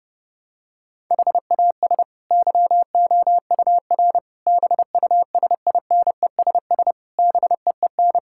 Morse code, heard as a keyed tone, about 30 wpm, 700 Hz. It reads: HAS YOUR BUSINESS BEEN
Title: Sentences from Top 1000 Words - Review Part 6 - 30wpm